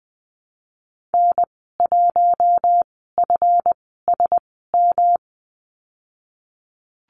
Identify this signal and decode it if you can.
Morse code, heard as a keyed tone, about 20 wpm, spaced as for 15 wpm, 700 Hz. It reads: N1FSM
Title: 173 Lesson 28g - Number 9 - Call Signs 20wpm